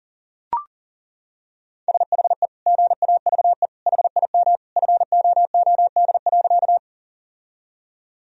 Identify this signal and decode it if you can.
Morse code, heard as a keyed tone, about 38 wpm, 700 Hz. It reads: SHE GAVE HIM FOOD.
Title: Sentences from Top 1000 Words - Review Part 1 - 40wpm